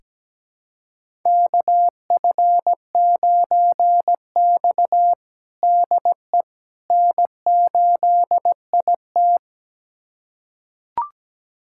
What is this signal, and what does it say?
Morse code, heard as a keyed tone, about 17 wpm, 700 Hz. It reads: KF9X DE N8IT
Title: USA Callsign DE Callsign - 4 Characters - 17wpm